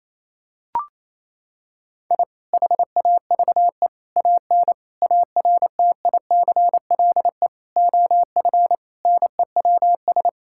Morse code, heard as a keyed tone, about 28 wpm, 700 Hz. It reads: I HAVE AN ARTICLE OF NEWS
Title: Sentences from Top 1000 Words - Review Part 5 - 28wpm